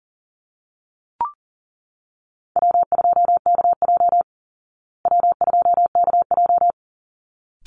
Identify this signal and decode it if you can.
Morse code, heard as a keyed tone, about 40 wpm, 700 Hz. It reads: W2KJ W2KJ
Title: Call Signs Repeated 3x - 40wpm